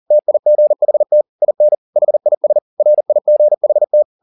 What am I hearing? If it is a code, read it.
Morse code, heard as a keyed tone, about 40 wpm, 600 Hz. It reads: TIGHT IN HIS RIGHT
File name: the old man and the sea 2 40wpm